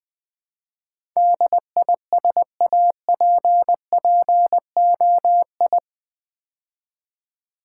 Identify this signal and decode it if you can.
Morse code, heard as a keyed tone, about 20 wpm, 700 Hz. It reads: DISAPPOI
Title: Build A Word - 4 Letters At A Time - 12-Letter Words - 20wpm